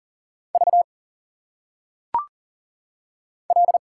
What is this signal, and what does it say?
Morse code, heard as a keyed tone, about 39 wpm, 700 Hz. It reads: V L
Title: Single Letters - Phonetic Pronunciation 40wpm